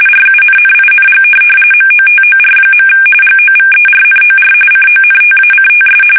RTTY, about 45.45 baud. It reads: RUNNING MOTOR. GOVERNOR CONTACTS